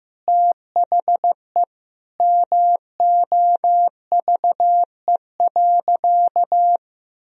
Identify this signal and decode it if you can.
Morse code, heard as a keyed tone, about 15 wpm, 700 Hz. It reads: THE MOVE.
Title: Sentences from Top 1000 Words - Review Part 1 - 15wpm